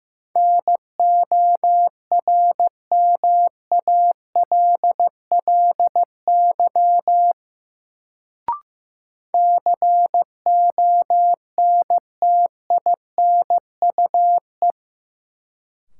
Morse code, heard as a keyed tone, about 15 wpm, 700 Hz. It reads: NORMALLY CONTINUE
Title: Sentences from Top 1000 Words - Review Part 6 - New Word Drill - 15wpm